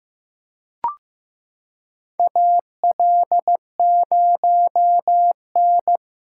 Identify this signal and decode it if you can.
Morse code, heard as a keyed tone, about 15 wpm, 700 Hz. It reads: AL0N
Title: USA Callsign DE Callsign - 4 Characters - 15wpm